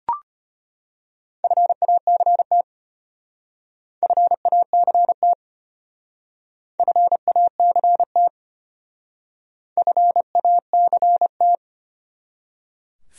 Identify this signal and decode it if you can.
Morse code, heard as a keyed tone, about 31 wpm, 700 Hz. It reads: FACT FACT FACT FACT
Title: Speed-Racing Top 500 Words 25wpm